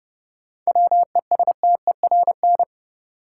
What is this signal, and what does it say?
Morse code, heard as a keyed tone, about 29 wpm, 700 Hz. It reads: WESTERN
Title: Sentences from Top 1000 Words - Review Part 7 - New Word Drill - 30wpm